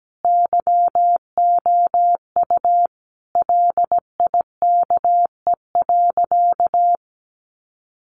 Morse code, heard as a keyed tone, about 17 wpm, 700 Hz. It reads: YOU LIKE.
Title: Sentences from Top 1400 Words 17wpm